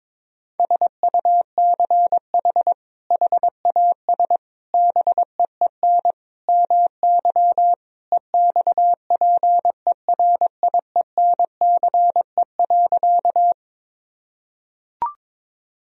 Morse code, heard as a keyed tone, about 22 wpm, 700 Hz. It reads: SUCH HAS BEEN MY EXPERIENCE.
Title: Sentences from Top 1000 Words - Review Part 3 - 22wpm